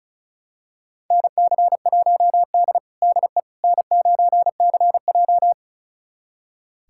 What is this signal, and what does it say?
Morse code, heard as a keyed tone, about 36 wpm, 700 Hz. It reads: NC1D DE N9CJ
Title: USA Callsign DE Callsign - 4 Characters - 35wpm